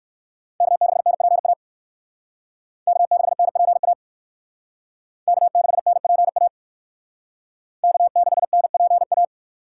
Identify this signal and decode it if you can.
Morse code, heard as a keyed tone, about 60 wpm, 700 Hz. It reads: K6NPA K6NPA K6NPA K6NPA